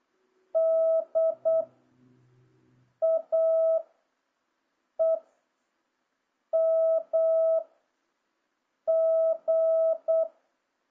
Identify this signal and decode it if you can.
Morse code, heard as a keyed tone, about 8 wpm, 650 Hz. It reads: D A E M G
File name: jh8cbh / 2026cw